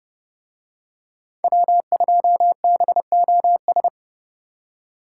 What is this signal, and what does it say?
Morse code, heard as a keyed tone, about 29 wpm, 700 Hz. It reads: W2BOS